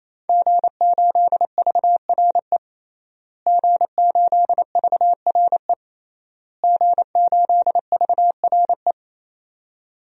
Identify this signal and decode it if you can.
Morse code, heard as a keyed tone, about 28 wpm, 700 Hz. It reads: G8VRE G8VRE G8VRE
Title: DXCC-100 Call Signs - 5 Characters - Repeated 3x - 28wpm